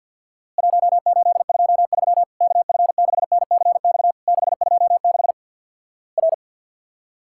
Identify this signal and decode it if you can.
Morse code, heard as a keyed tone, about 48 wpm, 700 Hz. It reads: 1823 KF6NCX 626 R
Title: Worked All Europe DX Contest - QTC Exchange with Repeats - 50wpm